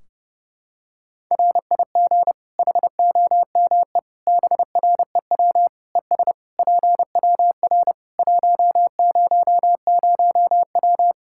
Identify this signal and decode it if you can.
Morse code, heard as a keyed tone, about 30 wpm, 700 Hz. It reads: RIG HOME BREW ES PWR 100W
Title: Practice QSO - Exchange 2 - Info - 30wpm